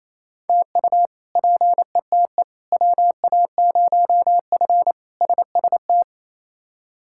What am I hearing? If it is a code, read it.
Morse code, heard as a keyed tone, about 28 wpm, 700 Hz. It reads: TU PETE WA0F SST
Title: K1USN SST QSO Practice - 28wpm